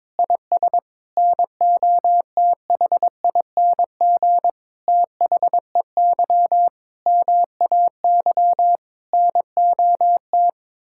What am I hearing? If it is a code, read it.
Morse code, heard as a keyed tone, about 22 wpm, 700 Hz. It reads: IS NOTHING THEY MAY NOT
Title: Sentences from Top 1000 Words - Review Part 6 - 22wpm